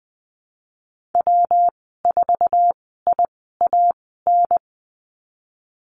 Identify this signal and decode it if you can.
Morse code, heard as a keyed tone, about 20 wpm, spaced as for 15 wpm, 700 Hz. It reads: W4IAN
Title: 147 Lesson 24g - Letter y - Call Signs 20wpm